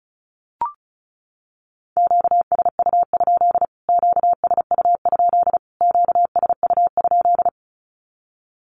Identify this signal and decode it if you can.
Morse code, heard as a keyed tone, about 34 wpm, 700 Hz. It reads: QSU? QSU? QSU?
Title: Q-Codes Repeated 3x 35wpm